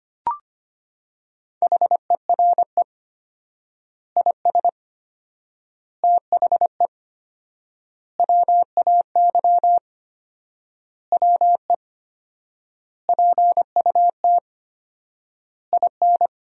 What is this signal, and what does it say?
Morse code, heard as a keyed tone, about 25 wpm, 700 Hz. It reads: HERE IS THE WAY WE PUT IN
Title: 4x Word-Spacing - Sentences from Top 200 Words - 25wpm